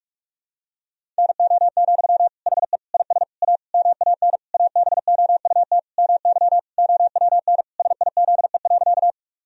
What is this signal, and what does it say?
Morse code, heard as a keyed tone, about 46 wpm, 700 Hz. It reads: NO, HE IS A MAN ABOUT MY OWN SIZE.
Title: Sentences from Top 1000 Words - Review Part 2 - 45wpm